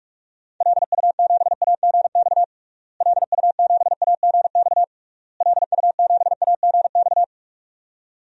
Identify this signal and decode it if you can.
Morse code, heard as a keyed tone, about 42 wpm, 700 Hz. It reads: LU7AGX LU7AGX LU7AGX
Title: DXCC-100 Call Signs - 6 Characters - Repeated 3x - 45wpm